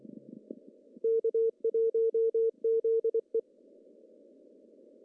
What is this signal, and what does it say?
Morse code, heard as a keyed tone, about 25 wpm, 450 Hz. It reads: K1ZE
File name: CW / cqww-cw-k5tr